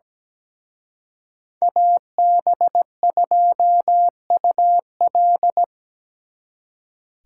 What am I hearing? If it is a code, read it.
Morse code, heard as a keyed tone, about 20 wpm, 700 Hz. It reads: AB2UL
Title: Speed Racing - USA 2x2 Call Signs 17wpm